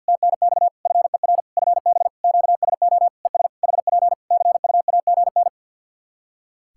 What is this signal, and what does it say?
Morse code, heard as a keyed tone, about 49 wpm, 700 Hz. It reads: TNX FER FB QSO ES HP CUAGN
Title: Practice QSO - Exchange 4 - Ending - 50wpm